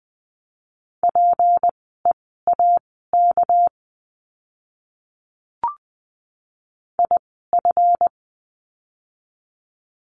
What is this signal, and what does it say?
Morse code, heard as a keyed tone, about 20 wpm, spaced as for 14 wpm, 700 Hz. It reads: PEAK IF
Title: 199 Lesson 32f - Letter k - Words 20wpm